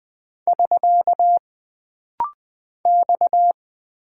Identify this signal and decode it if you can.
Morse code, heard as a keyed tone, about 20 wpm, 700 Hz. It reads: <SK> X
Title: Single Character (Letter:Number:Punctuation:Prosign) - Rapid-Fire - 20wpm